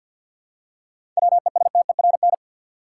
Morse code, heard as a keyed tone, about 48 wpm, 700 Hz. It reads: WESTERN
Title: Seven Letter Words - 50wpm